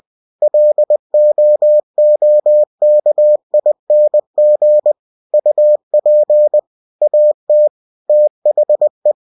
Morse code, heard as a keyed tone, about 20 wpm, 600 Hz. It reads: LOOKING UP AT THE
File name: winnie-20wpm